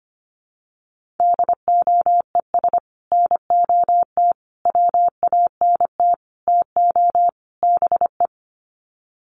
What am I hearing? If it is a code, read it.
Morse code, heard as a keyed tone, about 25 wpm, 700 Hz. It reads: DOES NOT WANT TO BE
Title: Sets of 5 Words Encore 25wpm